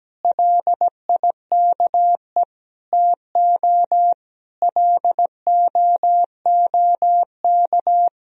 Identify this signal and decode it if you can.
Morse code, heard as a keyed tone, about 17 wpm, 700 Hz. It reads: LIKE TO LOOK
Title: Sentences from Top 200 Words and Abbreviations 17wpm